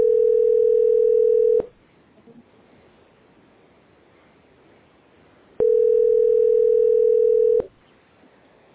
Ringback tone (North America), 440 + 480 Hz.